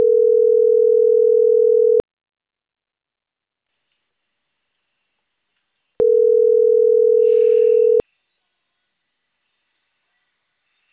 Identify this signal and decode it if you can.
Ringback tone (North America), 440 + 480 Hz.